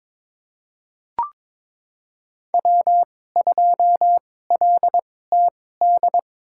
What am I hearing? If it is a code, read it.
Morse code, heard as a keyed tone, about 22 wpm, spaced as for 16 wpm, 700 Hz. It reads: W2LTD
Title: USA Call Signs - 2x Character Spacing - 22wpm